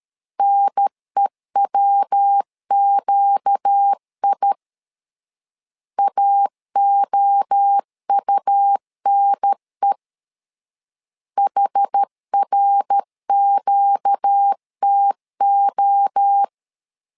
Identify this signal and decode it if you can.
Morse code, heard as a keyed tone, about 13 wpm, 800 Hz. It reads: NEWQI AOUNE HRQTO
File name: LEZIONI / 4 Q 12wpm 1833